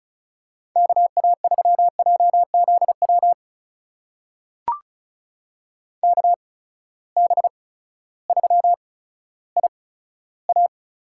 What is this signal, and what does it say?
Morse code, heard as a keyed tone, about 36 wpm, 700 Hz. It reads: K A 3 J Z W K B 3 I A
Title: USA Call Signs - 6 Characters - 8x Character Spacing - 35wpm